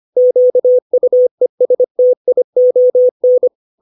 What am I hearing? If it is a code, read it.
Morse code, heard as a keyed tone, about 25 wpm, 500 Hz. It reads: QUESTION
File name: 632 words / 632 Words 25WPM